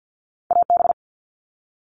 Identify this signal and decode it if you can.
Morse code, heard as a keyed tone, about 50 wpm, 700 Hz. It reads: AB
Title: Canadian Provinces and Territories - Repeated 3x - 50wpm